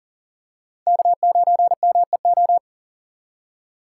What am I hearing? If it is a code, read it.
Morse code, heard as a keyed tone, about 40 wpm, 700 Hz. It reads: K9MEO
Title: USA Call Signs - 5 Characters - 6x Character Spacing - 40wpm